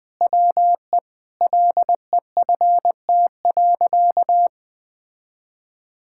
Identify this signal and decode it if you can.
Morse code, heard as a keyed tone, about 21 wpm, 700 Hz. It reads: WE LEFT.
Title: Sentences with 6-Word Phrases - 20wpm